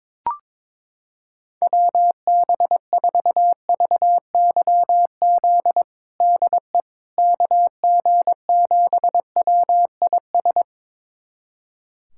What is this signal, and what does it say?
Morse code, heard as a keyed tone, about 22 wpm, 700 Hz. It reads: WB4VYZ DE KG7WIS
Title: USA Callsign DE Callsign 22wpm